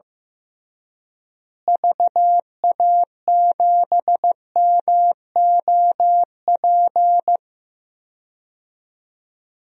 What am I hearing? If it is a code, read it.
Morse code, heard as a keyed tone, about 15 wpm, 700 Hz. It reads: VA7MOP
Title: DXCC-100 Call Signs - 6 Characters - Repeated 3x - 15wpm